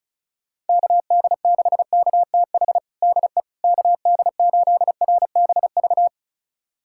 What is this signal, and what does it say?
Morse code, heard as a keyed tone, about 36 wpm, 700 Hz. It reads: KD6KTH DE KD8RBV
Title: USA Callsign DE Callsign - 6 Characters - 35wpm